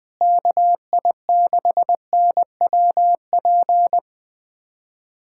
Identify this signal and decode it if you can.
Morse code, heard as a keyed tone, about 20 wpm, 700 Hz. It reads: KI6NWP
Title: USA Callsign DE Callsign - 6 Characters - 20wpm